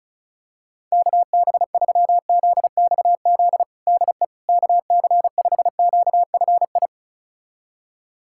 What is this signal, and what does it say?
Morse code, heard as a keyed tone, about 34 wpm, 700 Hz. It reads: KB3ZXZ DE KC5QFI
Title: USA Callsign DE Callsign - 6 Characters - 35wpm